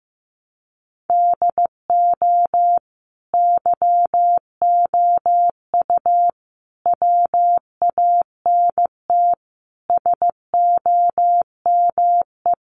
Morse code, heard as a keyed tone, about 14 wpm, 700 Hz. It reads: DO YOU WANT SOME
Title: Sentences from Top 300 Words 15wpm